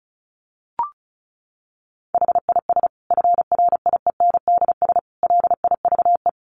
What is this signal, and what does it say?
Morse code, heard as a keyed tone, about 34 wpm, 700 Hz. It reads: HIS FRIENDS LIVE
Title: Additional Easy Phrases 35wpm